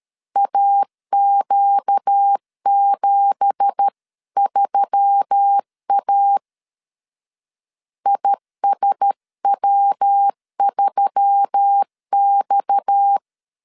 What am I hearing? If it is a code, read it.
Morse code, heard as a keyed tone, about 13 wpm, 800 Hz. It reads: AQ73A ISW3X
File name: CW / DET / 1 9 1500C S B 12wpm 3819